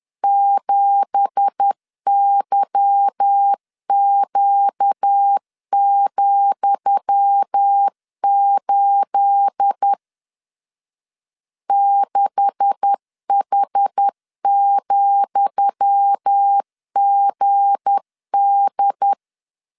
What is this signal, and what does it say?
Morse code, heard as a keyed tone, about 10 wpm, 800 Hz. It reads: 7YQ,8 6H,GD
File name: LEZIONI / 10 SIGNS / 10D COMA 10wpm 3013